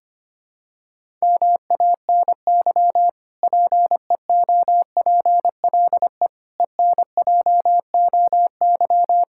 Morse code, heard as a keyed tone, about 25 wpm, 700 Hz. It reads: MANY PEOPLE ENJOY
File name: practice / 25wpm / Difficult Sentences with Hints - 25wpm